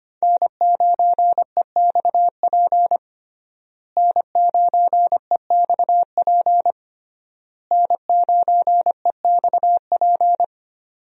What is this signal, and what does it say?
Morse code, heard as a keyed tone, about 25 wpm, 700 Hz. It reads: N9EXP N9EXP N9EXP
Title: USA Call Signs - 5 Characters - Repeated 3x - 25wpm